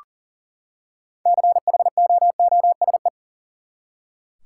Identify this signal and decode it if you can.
Morse code, heard as a keyed tone, about 40 wpm, 700 Hz. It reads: CHOOSE
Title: Six Letter Words - 40wpm